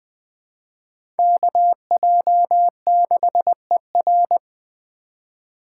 Morse code, heard as a keyed tone, about 20 wpm, 700 Hz. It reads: KJ6ER
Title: 2023 POTA Activator Call Signs - 20wpm